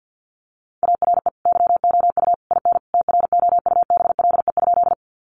Morse code, heard as a keyed tone, about 48 wpm, 700 Hz. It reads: ARE YOU IN TROUBLE?
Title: Sentences from Top 1200 Words 50wpm